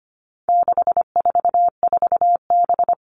Morse code, heard as a keyed tone, about 25 wpm, 700 Hz. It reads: 644B
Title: All Cut Numbers - 4-Digit 25wpm